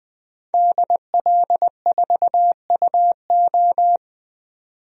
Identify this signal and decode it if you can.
Morse code, heard as a keyed tone, about 20 wpm, 700 Hz. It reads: DL4UO
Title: DXCC-100 Call Signs - 5 Characters - 20wpm